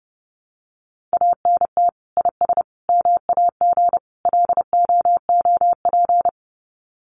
Morse code, heard as a keyed tone, about 30 wpm, 700 Hz. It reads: ANT IS MAG LOOP
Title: Common QSO Phrases 30wpm